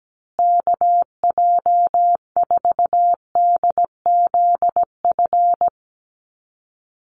Morse code, heard as a keyed tone, about 17 wpm, 700 Hz. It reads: KJ4DZF
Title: USA Call Signs Repeated 2x 17wpm